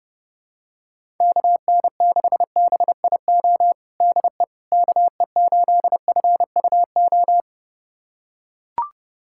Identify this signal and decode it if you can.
Morse code, heard as a keyed tone, about 30 wpm, 700 Hz. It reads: KN6BIO DE KE8FUO
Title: USA Callsign DE Callsign - 6 Characters - 30wpm